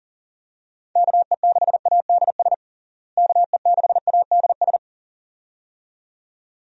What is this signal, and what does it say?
Morse code, heard as a keyed tone, about 40 wpm, 700 Hz. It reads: KE6ADS KE6ADS
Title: USA Call Signs - 6 Characters - Repeated 2x - 40wpm